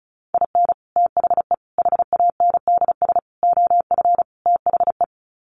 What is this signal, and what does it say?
Morse code, heard as a keyed tone, about 34 wpm, 700 Hz. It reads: IN THE HANDS OF THE
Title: Sets of 5 Words 35wpm